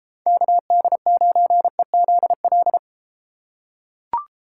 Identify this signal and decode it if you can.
Morse code, heard as a keyed tone, about 32 wpm, 700 Hz. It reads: KD9EZL